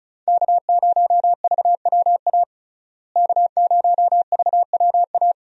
Morse code, heard as a keyed tone, about 36 wpm, 700 Hz. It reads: K0VWA K0VWA
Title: USA Call Signs - 5 Characters - Repeated 3x - 35wpm